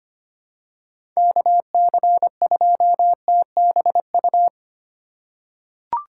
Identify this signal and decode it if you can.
Morse code, heard as a keyed tone, about 25 wpm, 700 Hz. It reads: KC2TBU
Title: USA Call Signs - 6 Characters - 6x Character Spacing - 25wpm